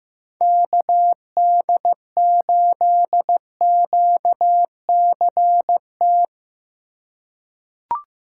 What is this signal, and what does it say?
Morse code, heard as a keyed tone, about 15 wpm, 700 Hz. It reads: KD8QCT
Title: USA Call Signs - 6 Characters - 3x Character Spacing - 15wpm